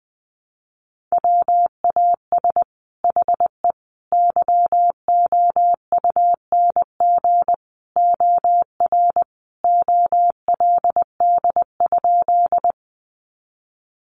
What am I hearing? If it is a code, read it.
Morse code, heard as a keyed tone, about 20 wpm, 700 Hz. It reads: WAS HE YOUNG OR OLD?
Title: Sentences from Top 400 Words 20wpm